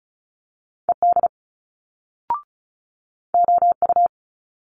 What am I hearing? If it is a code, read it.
Morse code, heard as a keyed tone, about 36 wpm, 700 Hz. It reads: ED OU
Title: Multi-letter Phonemes Quick Fire 35wpm